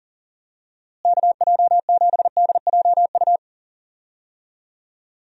Morse code, heard as a keyed tone, about 38 wpm, 700 Hz. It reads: KJ7DJU
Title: USA Call Signs - 6 Characters - Repeated 3x - 40wpm